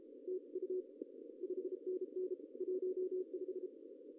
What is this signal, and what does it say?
Morse code, heard as a keyed tone, about 36 wpm, 410 Hz. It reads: TU 5NN 15
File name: CW / cqww-cw-k5tr